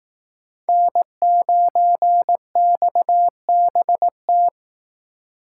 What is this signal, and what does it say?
Morse code, heard as a keyed tone, about 18 wpm, 700 Hz. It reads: N9XBT
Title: Speed Racing - USA 1x3 Call Signs 15wpm